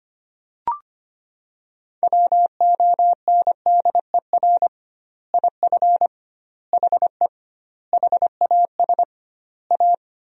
Morse code, heard as a keyed tone, about 25 wpm, 700 Hz. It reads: WONDER IF HE HAS A
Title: 2x Word-Spacing - Sentences from Top 500 Words - 25wpm